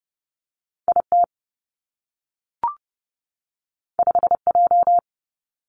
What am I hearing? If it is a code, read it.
Morse code, heard as a keyed tone, about 30 wpm, 700 Hz. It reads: I T 5 J
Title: Two Challenging Letter-Number Combinations 30wpm